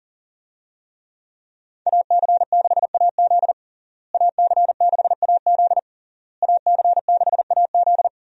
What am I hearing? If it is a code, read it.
Morse code, heard as a keyed tone, about 37 wpm, 700 Hz. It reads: AC6AZ AC6AZ AC6AZ
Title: USA Call Signs - 5 Characters - Repeated 3x - 40wpm